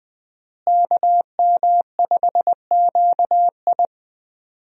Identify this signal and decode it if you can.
Morse code, heard as a keyed tone, about 20 wpm, 700 Hz. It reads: KM5QI
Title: USA Call Signs - 5 Characters - Repeated 3x - 20wpm